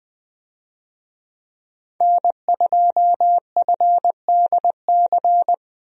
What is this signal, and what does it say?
Morse code, heard as a keyed tone, about 20 wpm, 700 Hz. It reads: N2FDC
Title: USA Call Signs - 5 Characters - Repeated 2x - 20wpm